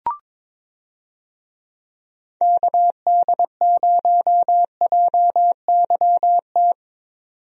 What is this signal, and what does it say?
Morse code, heard as a keyed tone, about 22 wpm, 700 Hz. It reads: KD0JYT
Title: USA Call Signs - 6 Characters - 22wpm